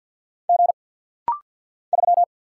Morse code, heard as a keyed tone, about 50 wpm, 700 Hz. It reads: G 3
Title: Single Letter-Number Alt - Rapid-Fire 50wpm